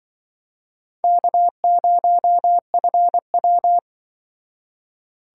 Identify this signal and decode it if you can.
Morse code, heard as a keyed tone, about 24 wpm, 700 Hz. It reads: K0FW
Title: Speed Racing - USA Call Signs 20wpm